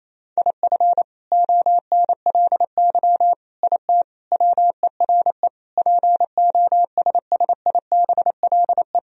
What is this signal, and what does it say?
Morse code, heard as a keyed tone, about 28 wpm, 700 Hz. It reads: IF ONLY IT WERE POSSIBLE
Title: Sentences from Top 500 Words and Abbreviations 28wpm